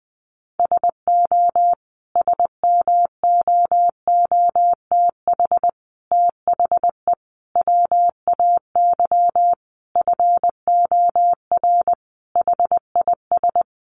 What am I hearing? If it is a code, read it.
Morse code, heard as a keyed tone, about 20 wpm, 700 Hz. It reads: SO SMOOTH THE WAY FOR HIS
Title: Sentences from Top 1400 Words 20wpm